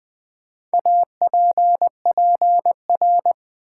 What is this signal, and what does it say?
Morse code, heard as a keyed tone, about 20 wpm, 700 Hz. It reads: APPR
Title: Build A Word - 4 Letters At A Time - 11-Letter Words - 20wpm